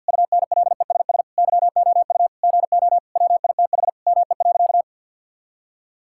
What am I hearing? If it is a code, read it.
Morse code, heard as a keyed tone, about 52 wpm, 700 Hz. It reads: UNLESS YOU GO WITH ME.
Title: Sentences from Top 1000 Words - Review Part 6 - 50wpm